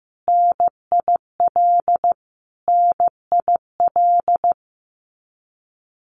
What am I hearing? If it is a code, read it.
Morse code, heard as a keyed tone, about 15 wpm, 700 Hz. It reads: NIL NIL
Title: Abbreviations Repeated 3x - 15wpm